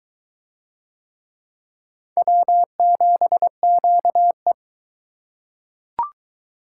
Morse code, heard as a keyed tone, about 21 wpm, 700 Hz. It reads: W7QE